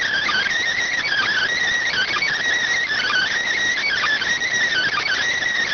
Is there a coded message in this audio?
Yes: Emergency Alert System (SAME) tones from 1.0 to 5.2 s, heard as harsh buzzing bursts.